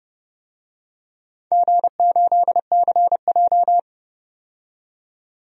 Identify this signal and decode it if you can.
Morse code, heard as a keyed tone, about 32 wpm, 700 Hz. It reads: G8CJ
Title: DXCC-100 Call Signs - 4 Characters - 30wpm